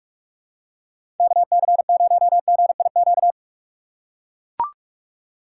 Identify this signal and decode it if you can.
Morse code, heard as a keyed tone, about 44 wpm, 700 Hz. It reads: KC0GIQ
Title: USA Call Signs - 6 Characters - 4x Character Spacing - 45wpm